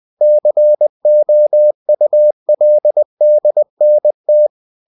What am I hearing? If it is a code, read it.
Morse code, heard as a keyed tone, about 20 wpm, 600 Hz. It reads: COULDNT
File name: cw-books / winnie-the-pooh / winnie-20wpm-page-0091